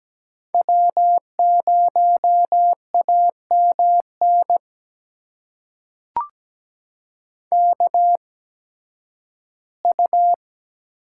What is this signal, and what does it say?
Morse code, heard as a keyed tone, about 17 wpm, spaced as for 11 wpm, 700 Hz. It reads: W0AMN KU
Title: USA Call Signs - 5 Characters - 8x Character Spacing - 17wpm